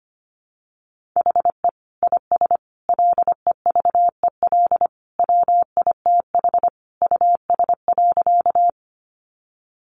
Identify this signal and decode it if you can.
Morse code, heard as a keyed tone, about 25 wpm, 700 Hz. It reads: HE IS LEVEL WITH US.
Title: Sentences from Top 700 Words 25wpm